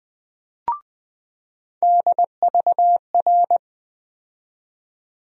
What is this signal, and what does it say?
Morse code, heard as a keyed tone, about 20 wpm, 700 Hz. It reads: DVR
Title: USA Call Sign Postfixes - 3 Letters 20wpm